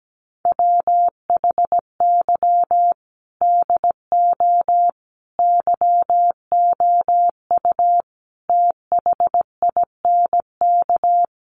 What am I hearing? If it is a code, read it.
Morse code, heard as a keyed tone, about 17 wpm, 700 Hz. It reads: WHY DO YOU THINK
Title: Sentences from Top 1400 Words 17wpm